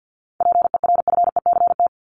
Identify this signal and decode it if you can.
Morse code, heard as a keyed tone, about 50 wpm, 700 Hz. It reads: PERFECT
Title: Top 751-1000 Words 50wpm